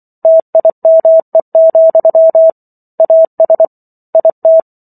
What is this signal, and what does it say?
Morse code, heard as a keyed tone, about 24 wpm, 650 Hz. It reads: TIME, AS IT